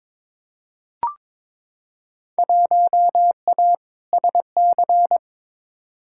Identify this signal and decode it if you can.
Morse code, heard as a keyed tone, about 22 wpm, 700 Hz. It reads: 1A SC
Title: Field Day Exchanges - 22wpm